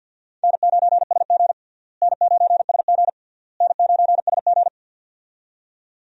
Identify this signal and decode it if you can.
Morse code, heard as a keyed tone, about 52 wpm, 700 Hz. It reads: N9SG N9SG N9SG
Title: USA Call Signs - 4 Characters - Repeated 3x - 50wpm